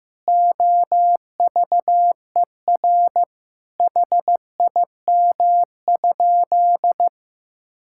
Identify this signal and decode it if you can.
Morse code, heard as a keyed tone, about 15 wpm, 700 Hz. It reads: OVER HIM?
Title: Sentences from Top 1600 Words 15wpm